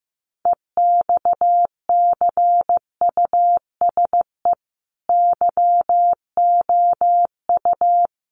Morse code, heard as a keyed tone, about 15 wpm, 700 Hz. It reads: EXCUSE YOU
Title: Sentences from Top 700 Words 15wpm